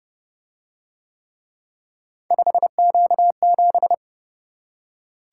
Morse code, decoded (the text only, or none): 5Q7